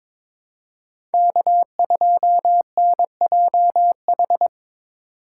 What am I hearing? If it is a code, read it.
Morse code, heard as a keyed tone, about 23 wpm, 700 Hz. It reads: K2NJH
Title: USA Call Signs - 5 Characters - 4x Character Spacing - 22wpm